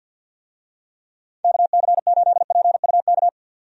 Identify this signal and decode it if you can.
Morse code, heard as a keyed tone, about 52 wpm, 700 Hz. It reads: KC8PUK
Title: USA Call Signs - 6 Characters - 50wpm